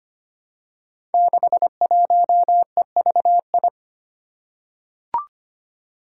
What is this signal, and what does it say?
Morse code, heard as a keyed tone, about 25 wpm, 700 Hz. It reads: 61EVI
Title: Five Challenging Letter-Number Combinations 25wpm